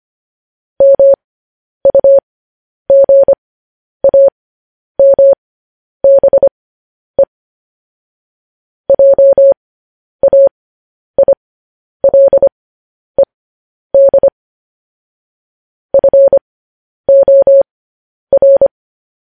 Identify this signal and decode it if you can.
Morse code, heard as a keyed tone, about 25 wpm, spaced as for 10 wpm, 550 Hz. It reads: MUGAMBE JAILED FOR
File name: podcast / News headlines-1x-25x10